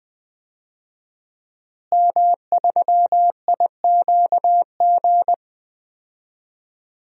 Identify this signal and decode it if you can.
Morse code, heard as a keyed tone, about 20 wpm, 700 Hz. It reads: M3IQG
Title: DXCC-100 Call Signs - 5 Characters - Repeated 2x - 20wpm